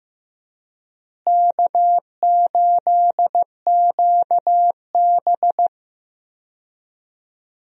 Morse code, heard as a keyed tone, about 15 wpm, 700 Hz. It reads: K8QB
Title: USA Call Signs - 4 Characters - Repeated 3x - 15wpm